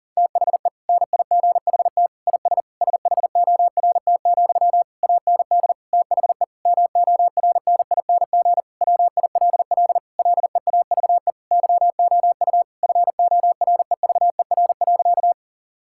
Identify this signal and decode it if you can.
Morse code, heard as a keyed tone, about 37 wpm, 700 Hz. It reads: THE NIGHT IS SHORT, AND THE MORNING WILL LEAVE YOU FOREVER.
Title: Sentences from Top 1600 Words 40wpm